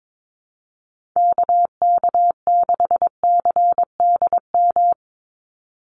Morse code, heard as a keyed tone, about 22 wpm, 700 Hz. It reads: KK6CDM
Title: USA Call Signs 22wpm